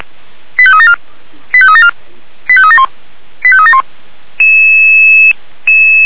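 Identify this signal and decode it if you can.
Five-tone selective call (ZVEI), heard as quick runs of five tones, tones of about 70 ms. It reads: 85374 85374 85371 85371
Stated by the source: ZVEI five-tone call 89894 received on SDR.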